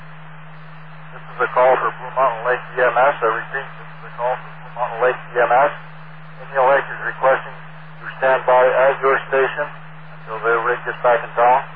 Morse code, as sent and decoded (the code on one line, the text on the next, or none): none